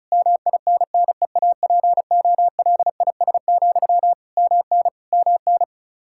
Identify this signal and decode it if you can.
Morse code, heard as a keyed tone, about 36 wpm, 700 Hz. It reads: MINNEAPOLIS, MN MN
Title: Practice QSO - Exchange 1 - Beginning - 35wpm